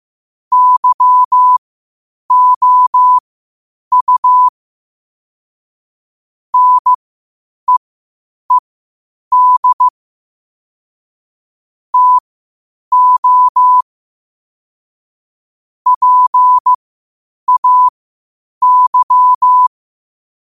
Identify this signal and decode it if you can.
Morse code, heard as a keyed tone, about 15 wpm, spaced as for 8 wpm, 1000 Hz. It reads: YOU NEED TO PAY
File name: qotdfiles / 8wpm